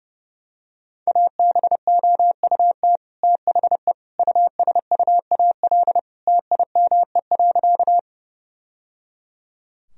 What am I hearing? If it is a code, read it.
Morse code, heard as a keyed tone, about 30 wpm, 700 Hz. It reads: ABOUT THE USUAL TIME.
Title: Sentences from Top 1000 Words - Review Part 2 - 30wpm